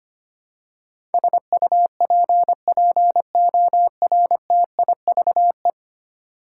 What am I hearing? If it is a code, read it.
Morse code, heard as a keyed tone, about 25 wpm, 700 Hz. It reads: SUPPORTIVE
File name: Ten Letter Words - 25wpm